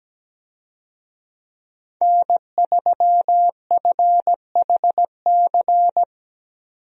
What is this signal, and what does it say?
Morse code, heard as a keyed tone, about 17 wpm, 700 Hz. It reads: N3FHC
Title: USA Call Signs - 5 Characters - 17wpm